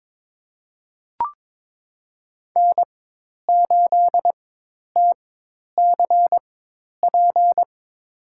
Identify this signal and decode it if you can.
Morse code, heard as a keyed tone, about 22 wpm, spaced as for 10 wpm, 700 Hz. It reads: N8TCP
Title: USA Call Signs - 5 Characters - 4x Character Spacing - 22wpm